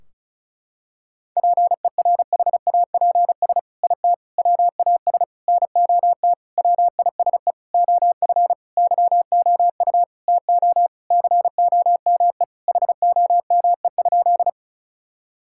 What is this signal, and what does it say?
Morse code, heard as a keyed tone, about 34 wpm, 700 Hz. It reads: PERHAPS IT WAS NOT WISE OF YOU TO COME HOME?
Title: Sentences from Top 1600 Words 35wpm